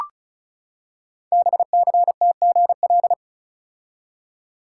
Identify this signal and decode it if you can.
Morse code, heard as a keyed tone, about 36 wpm, 700 Hz. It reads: BCTGL
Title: Code Groups - 5 Letters - 35wpm